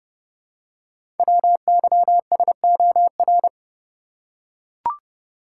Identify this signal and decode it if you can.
Morse code, heard as a keyed tone, about 30 wpm, 690 Hz. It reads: WYSOR
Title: Code Groups - 5 Letters - 30wpm